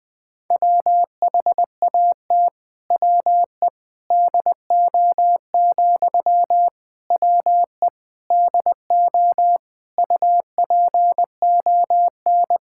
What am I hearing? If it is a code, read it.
Morse code, heard as a keyed tone, about 20 wpm, 700 Hz. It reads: WHAT WE DO, WE DO UPON
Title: Sentences from Top 1600 Words 20wpm